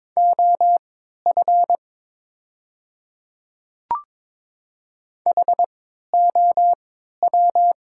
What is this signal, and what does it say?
Morse code, heard as a keyed tone, about 22 wpm, spaced as for 12 wpm, 690 Hz. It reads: OF HOW
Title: 3x Character Spacing - Top 100 Words - 22wpm